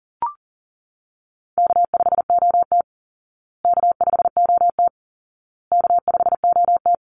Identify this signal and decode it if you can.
Morse code, heard as a keyed tone, about 40 wpm, 700 Hz. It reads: K5OT K5OT K5OT
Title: Call Signs Repeated 2x - 40wpm